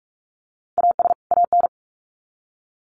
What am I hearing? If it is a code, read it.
Morse code, heard as a keyed tone, about 44 wpm, 700 Hz. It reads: AS AN
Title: Sets of 2 Words 45wpm